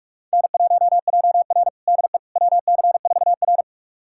Morse code, heard as a keyed tone, about 41 wpm, 700 Hz. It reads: N1JR DE WC4R
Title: USA Callsign DE Callsign - 4 Characters - 45wpm